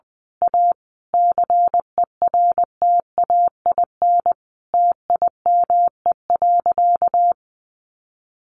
Morse code, heard as a keyed tone, about 20 wpm, 700 Hz. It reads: A CERTAIN TIME.
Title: Sentences from Top 1000 Words 20wpm